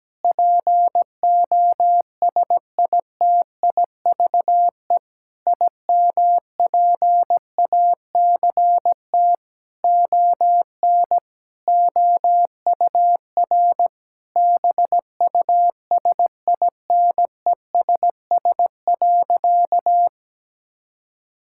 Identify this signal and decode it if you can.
Morse code, heard as a keyed tone, about 17 wpm, 700 Hz. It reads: POSITIVE IMPACT ON OUR BUSINESS.
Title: Sentences from Top 1000 Words - Review Part 6 - 17wpm